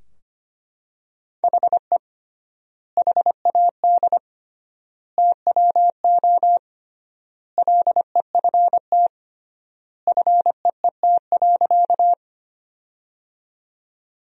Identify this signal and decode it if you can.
Morse code, heard as a keyed tone, about 25 wpm, 700 Hz. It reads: HE HAD TWO LEFT FEET.
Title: 3x Word-Spacing - Sentences from Top 300 Words - 25wpm